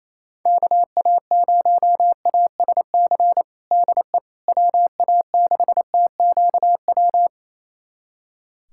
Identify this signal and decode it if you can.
Morse code, heard as a keyed tone, about 28 wpm, 700 Hz. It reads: KA0ASC DE WA6TQW
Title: USA Callsign DE Callsign - 6 Characters - 28wpm